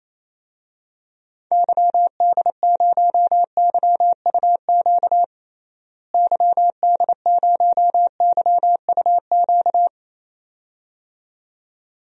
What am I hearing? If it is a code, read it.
Morse code, heard as a keyed tone, about 28 wpm, 700 Hz. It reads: YD0YUQ YD0YUQ
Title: DXCC-100 Call Signs - 6 Characters - Repeated 2x - 28wpm